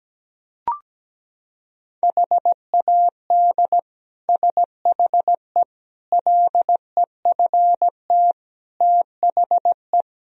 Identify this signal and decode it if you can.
Morse code, heard as a keyed tone, about 17 wpm, 700 Hz. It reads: HAD SHE LEFT THE
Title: Sentences from Top 1000 Words - Review Part 3 - 17wpm